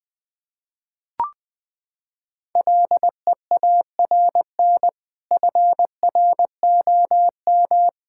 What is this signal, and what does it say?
Morse code, heard as a keyed tone, about 20 wpm, 700 Hz. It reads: LEARN FROM
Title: Sentences with 2-Word Phrases - 20wpm